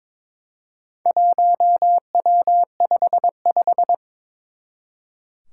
Morse code, heard as a keyed tone, about 22 wpm, 700 Hz. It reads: 1W55
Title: Four Challenging Letter-Number Combinations 22wpm